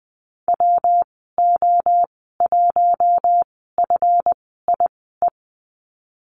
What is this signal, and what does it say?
Morse code, heard as a keyed tone, about 20 wpm, spaced as for 15 wpm, 700 Hz. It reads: WO1FIE
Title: 179 Lesson 29g - Slash - Call Signs 20wpm